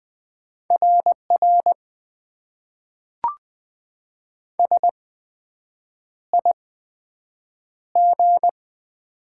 Morse code, heard as a keyed tone, about 20 wpm, spaced as for 7 wpm, 700 Hz. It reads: RR SIG
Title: Top 90 QSO Elements - 8x Character Spacing - 20wpm